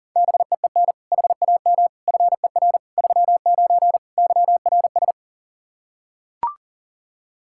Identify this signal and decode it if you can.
Morse code, heard as a keyed tone, about 37 wpm, 700 Hz. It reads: BEEN HAM FER 39 YRS
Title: K1USN SST QSO Practice 40wpm